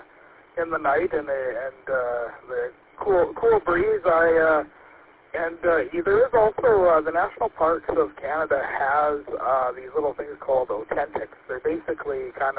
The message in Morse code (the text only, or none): none